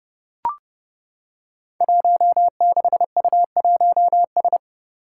Morse code, heard as a keyed tone, about 31 wpm, 700 Hz. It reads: 16U1S
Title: Five Challenging Letter-Number Combinations 30wpm